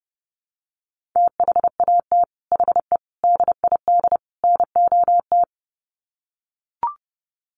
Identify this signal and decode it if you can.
Morse code, heard as a keyed tone, about 30 wpm, 700 Hz. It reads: THAT HE DID NOT
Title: Sets of 4 Words 30wpm